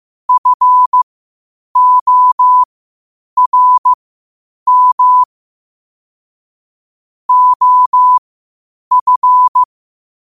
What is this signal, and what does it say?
Morse code, heard as a keyed tone, about 15 wpm, spaced as for 8 wpm, 1000 Hz. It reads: FORM OF
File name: qotdfiles / 8wpm